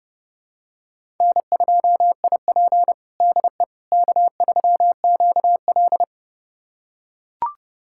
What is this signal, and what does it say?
Morse code, heard as a keyed tone, about 31 wpm, 700 Hz. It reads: N2IP DE K3QL
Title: USA Callsign DE Callsign - 4 Characters - 30wpm